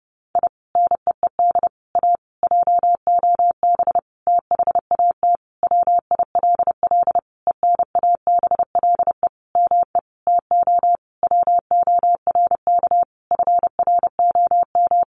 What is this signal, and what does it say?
Morse code, heard as a keyed tone, about 28 wpm, 700 Hz. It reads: I NEED A JOB THAT WILL ENABLE ME TO WORK FROM
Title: Sentences from Top 1500 Words 30wpm